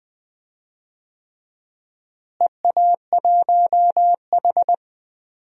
Morse code, heard as a keyed tone, about 20 wpm, 700 Hz. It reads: EA1H
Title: DXCC-100 Call Signs - 4 Characters - Repeated 3x - 20wpm